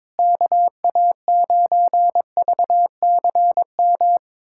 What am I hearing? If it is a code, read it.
Morse code, heard as a keyed tone, about 22 wpm, 700 Hz. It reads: KA9VCM
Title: USA Callsign DE Callsign 22wpm